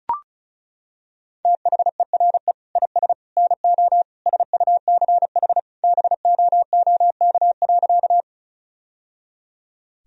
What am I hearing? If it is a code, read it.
Morse code, heard as a keyed tone, about 36 wpm, 700 Hz. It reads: THERE IS NO SUCH BOOK.
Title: Sentences from Top 1000 Words - Review Part 1 - 35wpm